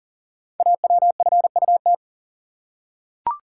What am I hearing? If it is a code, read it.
Morse code, heard as a keyed tone, about 38 wpm, 690 Hz. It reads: AWFUT
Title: Code Groups - 5 Letters - 40wpm